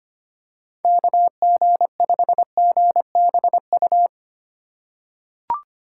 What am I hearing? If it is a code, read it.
Morse code, heard as a keyed tone, about 25 wpm, 700 Hz. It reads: KG5GBU
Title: USA Call Signs - 3x Character Spacing - 25wpm